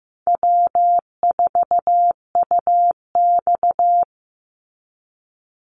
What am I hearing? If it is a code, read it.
Morse code, heard as a keyed tone, about 15 wpm, 700 Hz. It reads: W4UX
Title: Easy Call Signs Repeated 2x - 15wpm